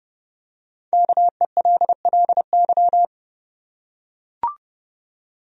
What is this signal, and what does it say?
Morse code, heard as a keyed tone, about 31 wpm, 700 Hz. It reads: KELLY
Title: USA Common Names - 30wpm